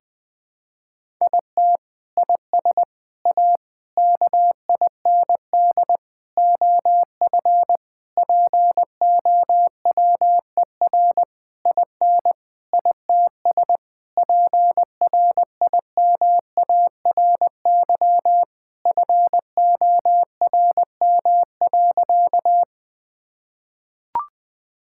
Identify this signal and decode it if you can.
Morse code, heard as a keyed tone, about 20 wpm, 700 Hz. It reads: IT IS A KIND OF POWER IN ITS PRIMARY FORM.
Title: Sentences from Top 1000 Words - Review Part 5 - 20wpm